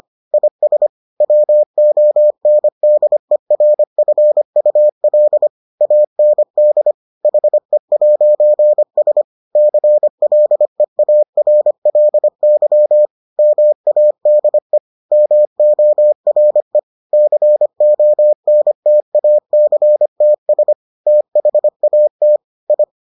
Morse code, heard as a keyed tone, about 25 wpm, 600 Hz. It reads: IS WONDERFUL AND HE'S CLEARLY MADE MORE CONTACTS THAT I